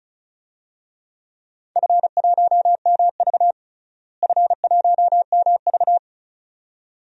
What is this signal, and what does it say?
Morse code, heard as a keyed tone, about 34 wpm, 700 Hz. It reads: F1MV F1MV
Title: DXCC-100 Call Signs - 4 Characters - Repeated 2x - 35wpm